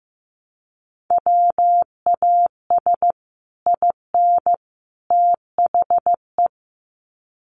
Morse code, heard as a keyed tone, about 15 wpm, 700 Hz. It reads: WAS IN THE
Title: Sets of 3 Words 15wpm